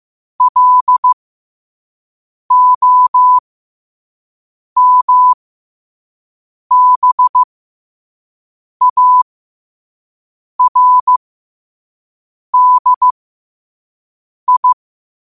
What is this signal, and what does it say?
Morse code, heard as a keyed tone, about 15 wpm, spaced as for 5 wpm, 1000 Hz. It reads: LOMBARDI